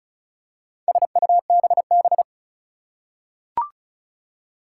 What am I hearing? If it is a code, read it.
Morse code, heard as a keyed tone, about 34 wpm, 700 Hz. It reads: SUBB
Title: Four Challenging Letter-Number Combinations 35wpm